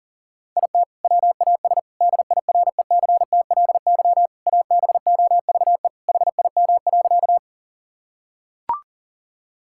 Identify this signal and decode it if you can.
Morse code, heard as a keyed tone, about 40 wpm, 700 Hz. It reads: IT WAS DIRECTLY ABOVE HIM.
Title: Sentences from Top 1000 Words - Review Part 5 - 40wpm